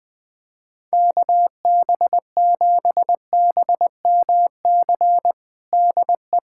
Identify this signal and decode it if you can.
Morse code, heard as a keyed tone, about 21 wpm, 700 Hz. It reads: KB7BMC DE
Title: USA Callsign DE Callsign - 6 Characters - 20wpm